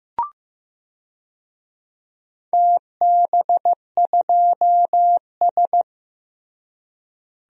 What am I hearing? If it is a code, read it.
Morse code, heard as a keyed tone, about 15 wpm, 700 Hz. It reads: TB2S
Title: DXCC-100 Call Signs - 4 Characters - Repeated 3x - 15wpm